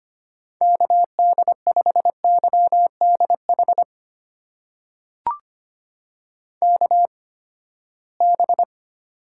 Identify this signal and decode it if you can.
Morse code, heard as a keyed tone, about 25 wpm, spaced as for 17 wpm, 700 Hz. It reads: KD5YDH KB
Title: USA Call Signs - 6 Characters - 8x Character Spacing - 25wpm